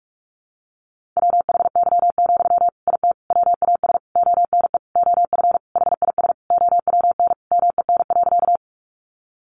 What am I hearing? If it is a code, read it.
Morse code, heard as a keyed tone, about 44 wpm, 700 Hz. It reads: WHY, IT WAS ONE OF HIS OWN MEN.
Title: Sentences from Top 300 Words 45wpm